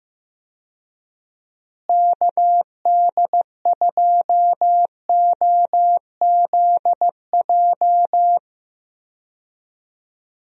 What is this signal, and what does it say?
Morse code, heard as a keyed tone, about 15 wpm, 700 Hz. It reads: KD2OZJ
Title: USA Call Signs - 6 Characters - Repeated 2x - 15wpm